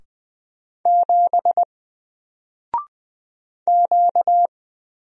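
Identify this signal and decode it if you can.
Morse code, heard as a keyed tone, about 20 wpm, 700 Hz. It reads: 7 Q
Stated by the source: Single letter and number Morse drill at 20 wpm, from N4LCD.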